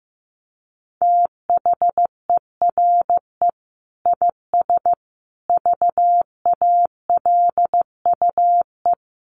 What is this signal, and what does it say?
Morse code, heard as a keyed tone, about 15 wpm, 700 Hz. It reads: THERE IS VALUE
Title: Sentences from Top 1100 Words 15wpm